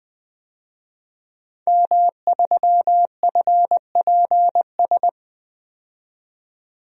Morse code, heard as a keyed tone, about 20 wpm, 700 Hz. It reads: M3FPS
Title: DXCC-100 Call Signs - 5 Characters - 20wpm